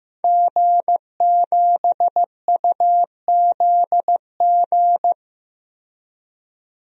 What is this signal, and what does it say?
Morse code, heard as a keyed tone, about 15 wpm, 700 Hz. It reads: G7UZG
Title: DXCC-100 Call Signs - 5 Characters - 15wpm